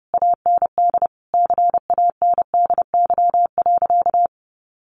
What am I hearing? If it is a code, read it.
Morse code, heard as a keyed tone, about 28 wpm, 700 Hz. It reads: AND CANDY.
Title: Easy Phrases 30wpm